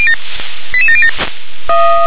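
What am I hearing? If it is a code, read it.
Five-tone selective call (ZVEI), heard as quick runs of five tones, tones of about 70 ms. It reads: 88707
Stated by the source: ZVEI five-tone selective call received on an SDR.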